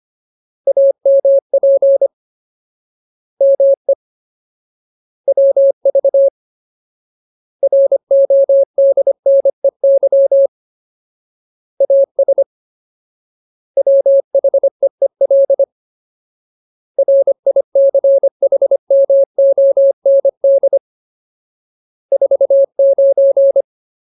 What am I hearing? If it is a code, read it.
Morse code, heard as a keyed tone, about 25 wpm, 550 Hz. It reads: AMP ME WV RODNEY AS WHEEL RICHMOND 49
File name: Ultimate Word List / 25WPM Ultimate